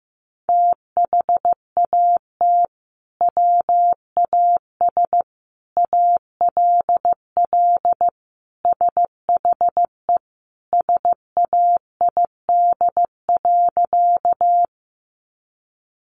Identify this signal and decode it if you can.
Morse code, heard as a keyed tone, about 15 wpm, 700 Hz. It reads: THAT WAS ALL SHE SAID.
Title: Sentences from Top 100 Words 15wpm